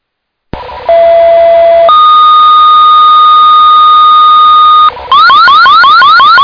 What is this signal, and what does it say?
Two-tone page, heard as a one-second tone, then a three-second tone, about 670 then 1230 Hz. It is heard from 0.9 to 4.9 s.